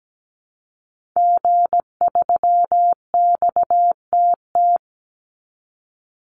Morse code, heard as a keyed tone, about 17 wpm, 700 Hz. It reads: G3XTT
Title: Call Signs Repeated 2x - 17wpm